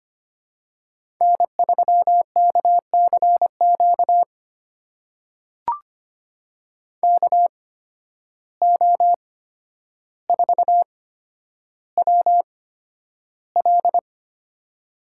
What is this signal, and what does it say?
Morse code, heard as a keyed tone, about 25 wpm, spaced as for 11 wpm, 700 Hz. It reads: N3KCQ KO4WL
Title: USA Call Signs - 5 Characters - 8x Character Spacing - 25wpm